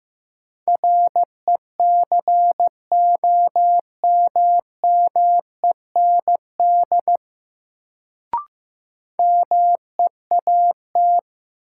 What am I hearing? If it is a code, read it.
Morse code, heard as a keyed tone, about 15 wpm, 700 Hz. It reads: RECOMMEND MEAT
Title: Sentences from Top 1200 Words - New Word Drill Alt - 15wpm